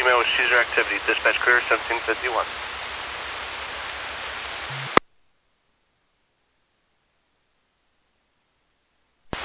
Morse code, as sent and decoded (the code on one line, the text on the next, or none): none